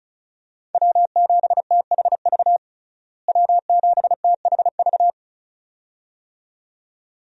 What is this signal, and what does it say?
Morse code, heard as a keyed tone, about 36 wpm, 700 Hz. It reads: W7THV W7THV
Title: USA Call Signs - 5 Characters - Repeated 2x - 35wpm